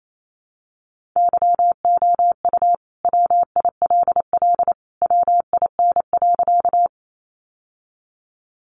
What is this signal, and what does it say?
Morse code, heard as a keyed tone, about 28 wpm, 700 Hz. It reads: YOU WILL WIN.
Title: Sentences from Top 1500 Words 28wpm